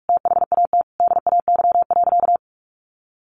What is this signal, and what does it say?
Morse code, heard as a keyed tone, about 44 wpm, 700 Hz. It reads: THAT DAY.
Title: Sentences from Top 1500 Words 45wpm